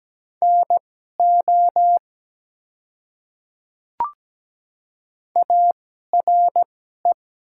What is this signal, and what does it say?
Morse code, heard as a keyed tone, about 17 wpm, spaced as for 12 wpm, 700 Hz. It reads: NO ARE
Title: 2x Character Spacing - Top 100 Words - 17wpm